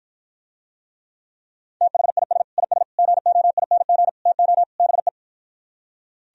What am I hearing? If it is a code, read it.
Morse code, heard as a keyed tone, about 52 wpm, 700 Hz. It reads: THIS IS GOING TO BE